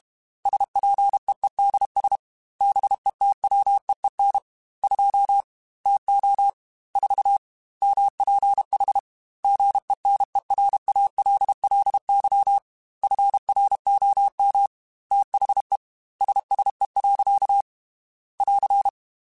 Morse code, heard as a keyed tone, about 32 wpm, 800 Hz. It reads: SPEEDS BETWEEN 2 TO 4 MPH GENERALLY FROM THE SSE. <AR>
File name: Weather Sets / Weather Sets-mp3s / Weather 4 14 @30